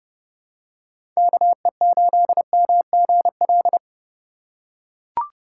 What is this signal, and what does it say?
Morse code, heard as a keyed tone, about 30 wpm, 700 Hz. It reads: KE8MGL
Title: USA Call Signs - 2x Character Spacing - 30wpm